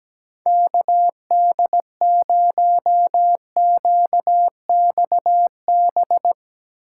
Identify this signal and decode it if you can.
Morse code, heard as a keyed tone, about 19 wpm, 700 Hz. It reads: KD0QXB
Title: Speed Racing - USA 2x3 Call Signs 17wpm